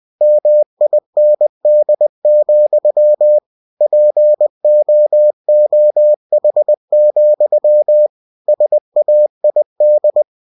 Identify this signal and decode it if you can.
Morse code, heard as a keyed tone, about 20 wpm, 600 Hz. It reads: MIND, POOH, SAID
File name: cw-books / winnie-the-pooh / winnie-20wpm-page-0063